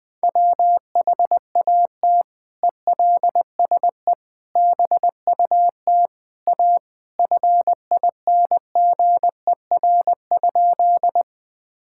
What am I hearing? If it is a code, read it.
Morse code, heard as a keyed tone, about 21 wpm, 700 Hz. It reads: WHAT ELSE BUT A FINGER?
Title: Sentences from Top 1600 Words 20wpm